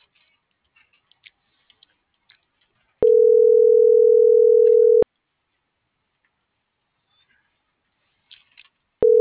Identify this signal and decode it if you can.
Ringback tone (North America), 440 + 480 Hz.
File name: CENTRAL TERRESTRE / out cris.f. - .49433